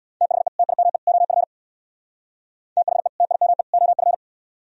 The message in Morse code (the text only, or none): THE TIME OF THE TIME OF